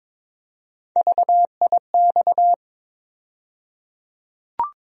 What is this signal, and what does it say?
Morse code, heard as a keyed tone, about 22 wpm, 700 Hz. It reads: VIX
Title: USA Call Sign Postfixes - 3 Letters 22wpm